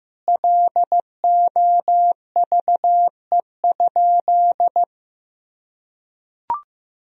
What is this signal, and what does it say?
Morse code, heard as a keyed tone, about 15 wpm, 700 Hz. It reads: LOVE?
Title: Sentences from Top 1000 Words - Review Part 1 - 15wpm